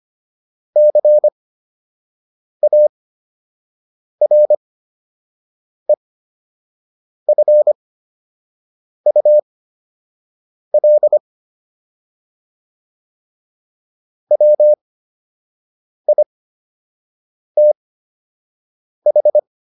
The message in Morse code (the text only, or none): CAREFUL WITH